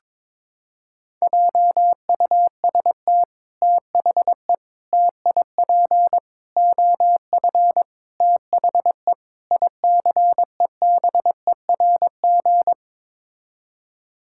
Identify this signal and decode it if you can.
Morse code, heard as a keyed tone, about 22 wpm, 700 Hz. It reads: JUST THE TIP OF THE ICEBERG
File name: practice / 22wpm / Sets of 6 Words - 22wpm